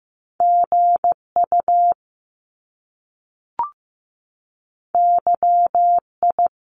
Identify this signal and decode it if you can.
Morse code, heard as a keyed tone, about 15 wpm, 700 Hz. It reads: GU YI
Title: Top Two Letter Combinations in Words 15wpm